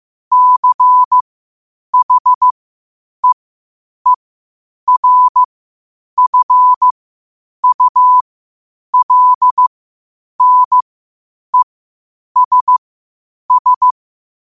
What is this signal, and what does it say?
Morse code, heard as a keyed tone, about 15 wpm, spaced as for 8 wpm, 1000 Hz. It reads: CHEERFULNESS
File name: qotdfiles / 8wpm